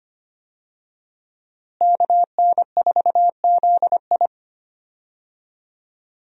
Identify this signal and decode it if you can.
Morse code, heard as a keyed tone, about 25 wpm, 700 Hz. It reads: KN4ZI
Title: USA Call Signs - 5 Characters - Repeated 2x - 25wpm